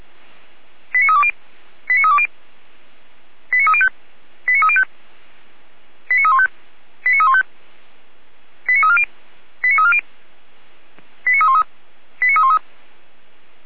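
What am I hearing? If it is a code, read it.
Five-tone selective call (ZVEI), heard as quick runs of five tones, tones of about 70 ms. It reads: 89320 89320 89305 89305 89315 89315 89340 89340 89313 89313
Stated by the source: ZVEI five-tone call 89510 received by SDR.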